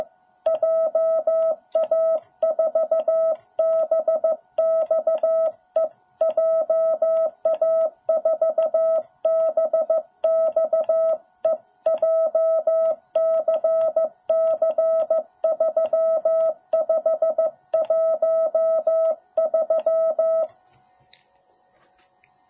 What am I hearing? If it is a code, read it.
Morse code, heard as a keyed tone, about 14 wpm, 660 Hz. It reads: JA4BXE JA4BXE JCC3513
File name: jh8cbh / 2021cw